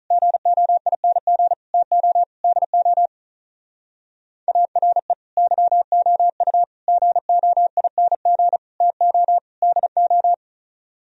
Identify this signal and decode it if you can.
Morse code, heard as a keyed tone, about 44 wpm, 700 Hz. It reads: GOING TO DO ARE YOU GOING TO DO
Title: Speed-Racing - Sets of 5 Words 35wpm